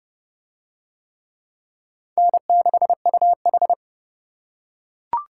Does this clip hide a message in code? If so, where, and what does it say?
Yes: Morse code from 2.2 to 3.7 s, heard as a keyed tone, about 26 wpm, 700 Hz. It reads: N6UH